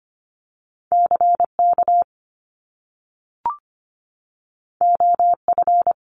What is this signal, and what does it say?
Morse code, heard as a keyed tone, about 25 wpm, 700 Hz. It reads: CK OF
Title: Top Two Letter Combinations in Words 25wpm